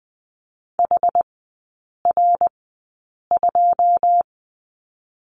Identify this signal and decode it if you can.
Morse code, heard as a keyed tone, about 20 wpm, spaced as for 8 wpm, 700 Hz. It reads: HR2
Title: 081 Lesson 14e - Number 2 - Three Characters 20wpm